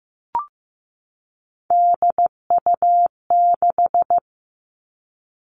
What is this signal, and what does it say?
Morse code, heard as a keyed tone, about 15 wpm, 700 Hz. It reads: DU6
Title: All Cut Numbers - 3-Digit 15wpm